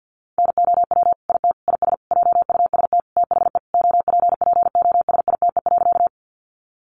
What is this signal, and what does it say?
Morse code, heard as a keyed tone, about 48 wpm, 700 Hz. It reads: NOW IT IS JUST THE OPPOSITE.
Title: Sentences from Top 1300 Words 50wpm